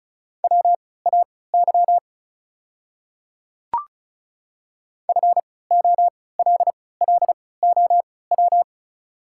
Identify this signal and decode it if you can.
Morse code, heard as a keyed tone, about 35 wpm, spaced as for 20 wpm, 690 Hz. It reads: WAY FOLLOW
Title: 3x Character Spacing - Top 200 Words - 35wpm